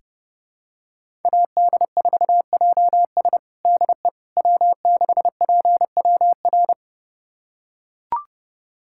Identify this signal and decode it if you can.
Morse code, heard as a keyed tone, about 32 wpm, 700 Hz. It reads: AD4JS DE W6PWR
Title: USA Callsign DE Callsign - 5 Characters - 30wpm